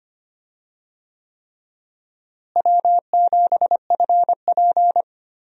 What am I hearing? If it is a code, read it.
Morse code, heard as a keyed tone, about 25 wpm, 700 Hz. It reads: W7FP
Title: USA Call Signs - 4 Characters - Repeated 3x - 25wpm